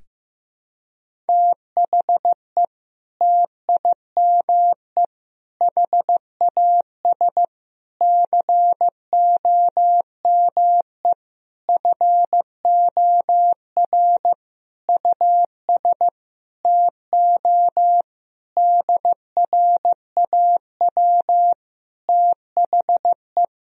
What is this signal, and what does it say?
Morse code, heard as a keyed tone, about 15 wpm, 700 Hz. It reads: THE TIME HAS COME FOR US TO DRAW THE
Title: Sentences from Top 1000 Words - Review Part 1 - 15wpm